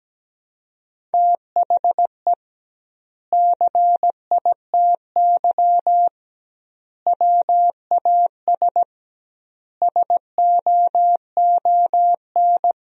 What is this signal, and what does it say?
Morse code, heard as a keyed tone, about 17 wpm, 700 Hz. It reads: THE CITY WAS SOON
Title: 2x Word-Spacing - Sentences from Top 500 Words - 17wpm